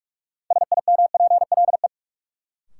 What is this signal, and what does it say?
Morse code, heard as a keyed tone, about 45 wpm, 700 Hz. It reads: SIMPLE
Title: Sentences from Top 400 Words - New Word Drill - 45wpm